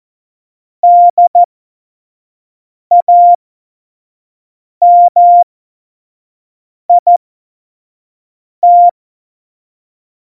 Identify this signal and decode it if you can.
Morse code, heard as a keyed tone, about 14 wpm, spaced as for 5 wpm, 700 Hz. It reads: DAMIT
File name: Morsekurs / Witze / Witz 14 Juli 2020, 25 F70